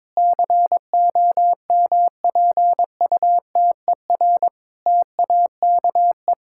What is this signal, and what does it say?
Morse code, heard as a keyed tone, about 22 wpm, 700 Hz. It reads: COMPUTER TAKE
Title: Sentences from Top 1000 Words - Review Part 6 - 22wpm